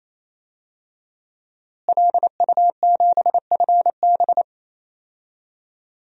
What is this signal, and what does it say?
Morse code, heard as a keyed tone, about 28 wpm, 700 Hz. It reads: LU7FB
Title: DXCC-100 Call Signs - 5 Characters - Repeated 3x - 28wpm